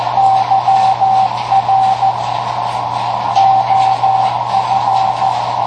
Morse code, heard as a keyed tone, about 14 wpm, 770 Hz, with radio noise. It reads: YR OF
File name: tracks / KA9SZX 3W Macomb IL